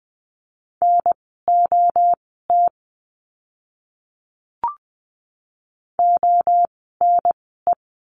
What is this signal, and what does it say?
Morse code, heard as a keyed tone, about 20 wpm, spaced as for 14 wpm, 700 Hz. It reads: NOT ONE
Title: 030 Lesson 6f - Letter i - Words 20wpm